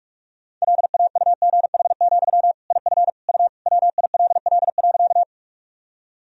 Morse code, heard as a keyed tone, about 46 wpm, 700 Hz. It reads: LAUGH, IF U WILL.
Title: Sentences from Top 500 Words and Abbreviations 45wpm